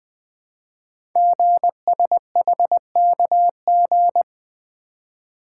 Morse code, heard as a keyed tone, about 20 wpm, 700 Hz. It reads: GSHKG
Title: Code Groups - 5 Letters - 20wpm